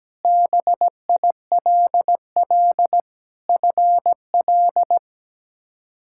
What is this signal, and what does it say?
Morse code, heard as a keyed tone, about 17 wpm, 700 Hz. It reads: BILL FL
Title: K1USN SST QSO Practice - 17wpm